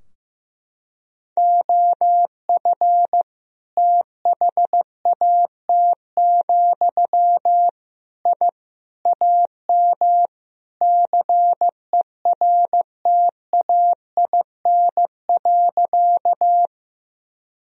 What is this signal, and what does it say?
Morse code, heard as a keyed tone, about 15 wpm, 700 Hz. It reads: OF THAT, I AM CERTAIN.
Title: Sentences from Top 1000 Words - Review Part 2 - 15wpm